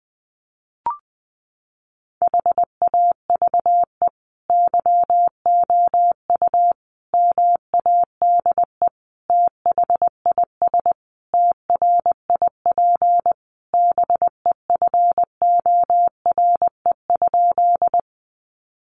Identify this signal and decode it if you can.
Morse code, heard as a keyed tone, about 20 wpm, 700 Hz. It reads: HAVE YOU MADE THIS TRIP BEFORE?
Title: Sentences from Top 1300 Words 20wpm